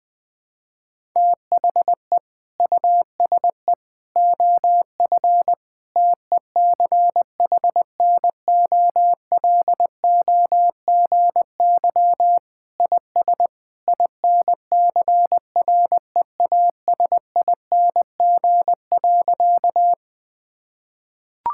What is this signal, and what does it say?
Morse code, heard as a keyed tone, about 20 wpm, 700 Hz. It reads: THE USE OF TECHNOLOGY IS INCREASING.
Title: Sentences with 2-Word Phrases - Spoken Beginning - 20wpm